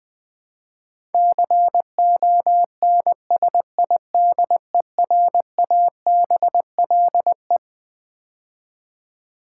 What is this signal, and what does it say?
Morse code, heard as a keyed tone, about 20 wpm, 700 Hz. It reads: CONSIDERABLE
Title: Build A Word - 6 Letters At A Time - 12-Letter Words - 20wpm